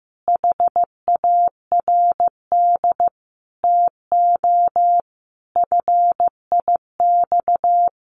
Morse code, heard as a keyed tone, about 15 wpm, 700 Hz. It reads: HARD TO FIX
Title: Sentences from Top 1400 Words 15wpm